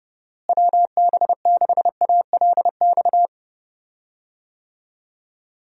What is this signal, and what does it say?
Morse code, heard as a keyed tone, about 32 wpm, 700 Hz. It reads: WB6ALX
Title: USA Call Signs - 6 Characters - Repeated 3x - 30wpm